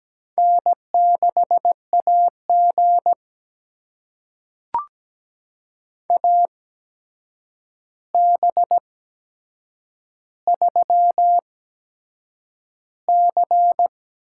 Morse code, heard as a keyed tone, about 17 wpm, spaced as for 7 wpm, 700 Hz. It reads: N6AG AB3C
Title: USA Call Signs - 8x Character Spacing - 17wpm